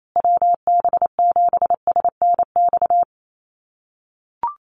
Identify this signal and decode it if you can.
Morse code, heard as a keyed tone, about 28 wpm, 700 Hz. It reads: WB7SNX
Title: USA Call Signs Repeated 3x 28wpm